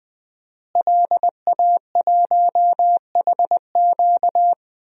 Morse code, heard as a keyed tone, about 20 wpm, 700 Hz. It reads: LA1HQ
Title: DXCC-100 Call Signs - 5 Characters - Repeated 3x - 20wpm